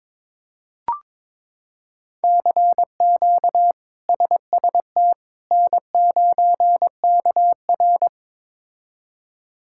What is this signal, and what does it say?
Morse code, heard as a keyed tone, about 22 wpm, 700 Hz. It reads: CQ SST N9KR
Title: K1USN SST QSO Practice - 22wpm